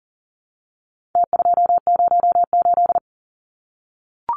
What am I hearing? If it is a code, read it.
Morse code, heard as a keyed tone, about 40 wpm, 700 Hz. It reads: T208
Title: Numbers - Contest Style - 4-Digit 40wpm